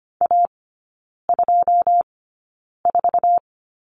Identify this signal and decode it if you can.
Morse code, heard as a keyed tone, about 25 wpm, spaced as for 9 wpm, 700 Hz. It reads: A24
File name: Lesson 14e - Number 2 - Three Characters 25wpm